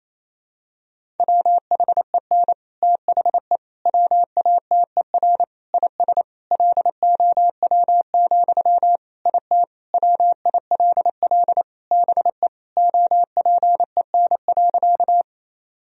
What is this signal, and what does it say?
Morse code, heard as a keyed tone, about 28 wpm, 700 Hz. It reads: WHEN THE WATER IS LOW, IT WILL BE OPEN.
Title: Sentences from Top 1000 Words - Review Part 1 - 28wpm